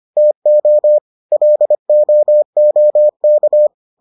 Morse code, heard as a keyed tone, about 25 wpm, 600 Hz. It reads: TO LOOK